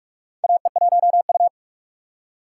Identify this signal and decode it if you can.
Morse code, heard as a keyed tone, about 46 wpm, 700 Hz. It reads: AE1U
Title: USA Call Signs - 4 Characters - 6x Character Spacing - 45wpm